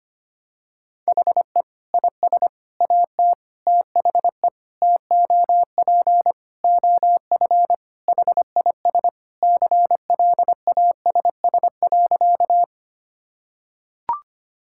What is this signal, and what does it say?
Morse code, heard as a keyed tone, about 25 wpm, 700 Hz. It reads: HE IS AT THE TOP OF HIS CLASS.
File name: practice / 25wpm / Sentences with 4-Word Phrases - 25wpm